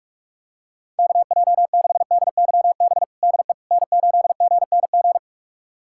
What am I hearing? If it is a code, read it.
Morse code, heard as a keyed tone, about 45 wpm, 700 Hz. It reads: KJ6DYB DE N8GNG
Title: USA Callsign DE Callsign 45wpm